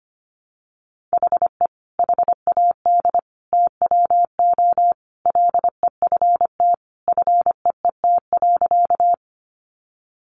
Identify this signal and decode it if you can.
Morse code, heard as a keyed tone, about 25 wpm, 700 Hz. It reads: HE HAD TWO LEFT FEET.
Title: Sentences from Top 300 Words 25wpm